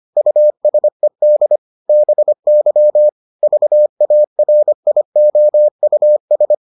Morse code, heard as a keyed tone, about 25 wpm, 600 Hz. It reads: USED BY VARIOUS